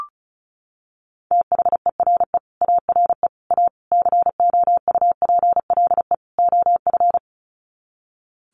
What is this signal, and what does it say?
Morse code, heard as a keyed tone, about 34 wpm, 700 Hz. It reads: THERE ARE A COUPLE OF
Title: Sets of 5 Words Encore 35wpm